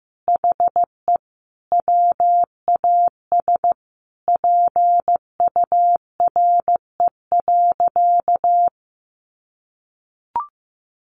Morse code, heard as a keyed tone, about 15 wpm, 700 Hz. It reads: HE WAS PURE.
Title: Sentences from Top 1500 Words 15wpm